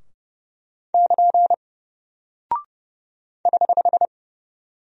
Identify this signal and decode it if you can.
Morse code, heard as a keyed tone, about 30 wpm, 700 Hz. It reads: <KN> <HH>
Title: Single Character (Letter:Number:Punctuation:Prosign) Alt 30wpm